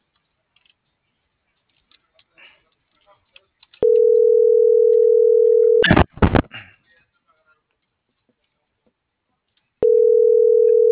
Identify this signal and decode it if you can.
Ringback tone (North America), 440 + 480 Hz.